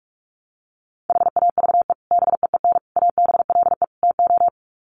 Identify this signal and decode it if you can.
Morse code, heard as a keyed tone, about 44 wpm, 700 Hz. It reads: HAVE BEEN ABLE TO
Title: Sets of 4 Words 45wpm